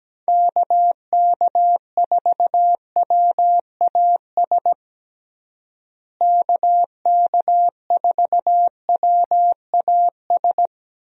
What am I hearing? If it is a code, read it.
Morse code, heard as a keyed tone, about 17 wpm, 700 Hz. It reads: KK4WAS KK4WAS
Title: USA Call Signs - 6 Characters - Repeated 3x - 17wpm